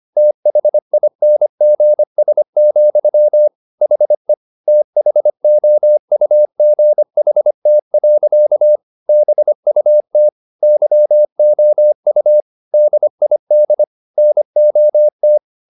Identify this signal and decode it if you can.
Morse code, heard as a keyed tone, about 25 wpm, 600 Hz. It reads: THINGS, HE THOUGHT. BUT YOU DID NOT